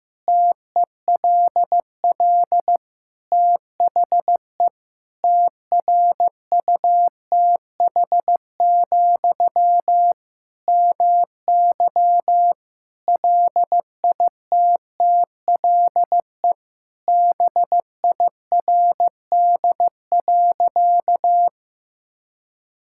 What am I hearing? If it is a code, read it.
Morse code, heard as a keyed tone, about 15 wpm, 700 Hz. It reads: TELL THE TRUTH, MY LITTLE BIRD.
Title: Sentences from Top 1000 Words - Review Part 6 - 15wpm